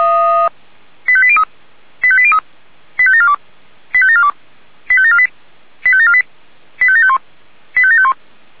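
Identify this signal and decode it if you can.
Five-tone selective call (ZVEI), heard as quick runs of five tones, tones of about 70 ms. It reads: 85803 85803 85742 85742 85748 85748 85751 85751
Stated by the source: ZVEI five-tone call received by SDR.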